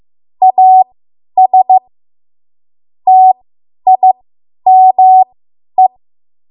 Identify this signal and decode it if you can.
Morse code, heard as a keyed tone, about 15 wpm, spaced as for 10 wpm, 750 Hz. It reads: AS TIME